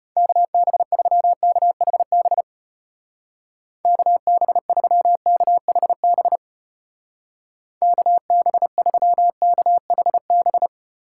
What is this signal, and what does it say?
Morse code, heard as a keyed tone, about 31 wpm, 700 Hz. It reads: KB3KHB KB3KHB KB3KHB
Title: Speed Racing - USA Call Signs 25wpm